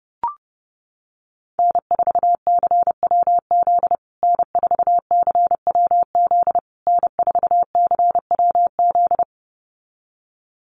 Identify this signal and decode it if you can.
Morse code, heard as a keyed tone, about 28 wpm, 700 Hz. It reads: N4CWZ N4CWZ N4CWZ
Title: USA Call Signs Repeated 3x 30wpm